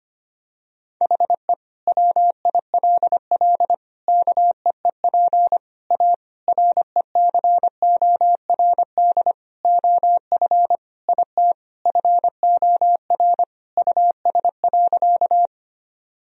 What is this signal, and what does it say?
Morse code, heard as a keyed tone, about 25 wpm, 700 Hz. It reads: HE WILL KEEP A RECORD OF IT FOR US.
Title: Sentences from Top 1000 Words - Review Part 3 - 25wpm